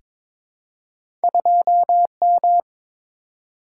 Morse code, heard as a keyed tone, about 22 wpm, 700 Hz. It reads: 2M
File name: practice / 22wpm / Two Letter-Number Combinations - 22wpm